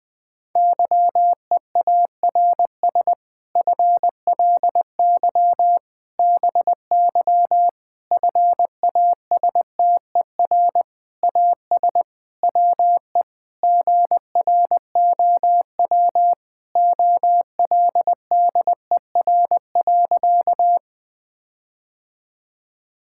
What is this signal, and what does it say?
Morse code, heard as a keyed tone, about 20 wpm, 700 Hz. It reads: YEARS FLY BY FASTER AS WE GROW OLDER.
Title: Difficult Sentences with Hints - 20wpm